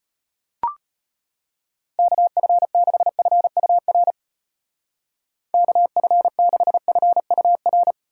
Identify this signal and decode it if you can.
Morse code, heard as a keyed tone, about 31 wpm, 700 Hz. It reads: KF6FUR KF6FUR
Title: Speed Racing - USA 2x3 Call Signs 25wpm